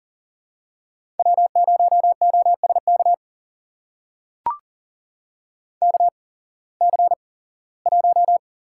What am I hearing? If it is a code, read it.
Morse code, heard as a keyed tone, about 40 wpm, spaced as for 21 wpm, 700 Hz. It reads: W0OSK KC1
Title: USA Call Signs - 5 Characters - 8x Character Spacing - 40wpm